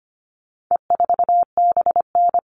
Morse code, heard as a keyed tone, about 25 wpm, 700 Hz. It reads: E4BN
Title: All Cut Numbers - 4-Digit 25wpm